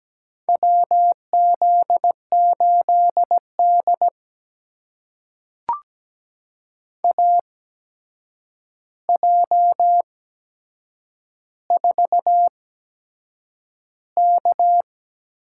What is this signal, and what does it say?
Morse code, heard as a keyed tone, about 17 wpm, spaced as for 7 wpm, 700 Hz. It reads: WZ8D AJ4K
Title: USA Call Signs - 8x Character Spacing - 17wpm